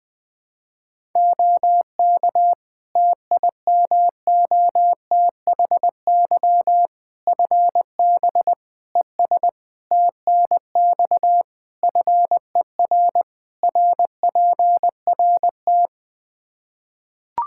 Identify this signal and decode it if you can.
Morse code, heard as a keyed tone, about 20 wpm, 700 Hz. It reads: OK TIMOTHY FB ES TNX FER RPRT
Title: Practice QSO - Exchange 2 - Info - 20wpm